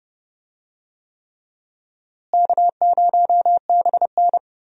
Morse code, heard as a keyed tone, about 32 wpm, 700 Hz. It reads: K0BN
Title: USA Call Signs - 4 Characters - Repeated 3x - 30wpm